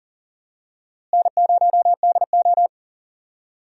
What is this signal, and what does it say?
Morse code, heard as a keyed tone, about 40 wpm, 700 Hz. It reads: N0DO
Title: USA Call Signs - 4 Characters - Repeated 2x - 40wpm